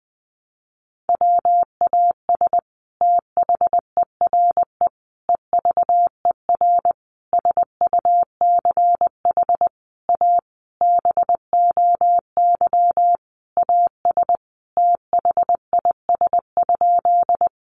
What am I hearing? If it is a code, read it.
Morse code, heard as a keyed tone, about 20 wpm, 700 Hz. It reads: WAS THERE EVER SUCH A BOY AS THIS?
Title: Sentences from Top 400 Words 20wpm